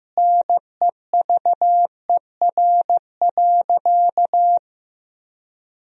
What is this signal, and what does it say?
Morse code, heard as a keyed tone, about 15 wpm, 700 Hz. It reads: NEVER.
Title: 3x Word-Spacing - Sentences from Top 300 Words - 15wpm